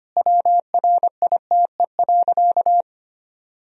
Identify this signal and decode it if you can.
Morse code, heard as a keyed tone, about 25 wpm, 700 Hz. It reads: WRITE.
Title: Sentences from Top 1000 Words - Review Part 1 - 25wpm